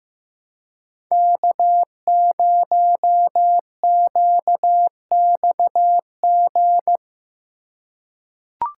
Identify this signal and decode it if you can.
Morse code, heard as a keyed tone, about 15 wpm, 700 Hz. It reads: K0QXG
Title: USA Call Signs - 5 Characters - 4x Character Spacing - 15wpm